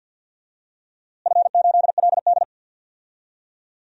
Morse code, heard as a keyed tone, about 50 wpm, 700 Hz. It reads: F8LD
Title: DXCC-100 Call Signs - 4 Characters - Repeated 3x - 50wpm